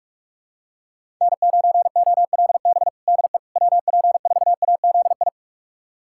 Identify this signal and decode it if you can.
Morse code, heard as a keyed tone, about 46 wpm, 700 Hz. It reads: N9OLB DE WP4AZI
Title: USA Callsign DE Callsign 45wpm